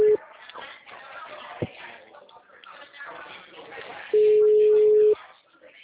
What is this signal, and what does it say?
Ringback tone (Europe, 425 Hz), with speech.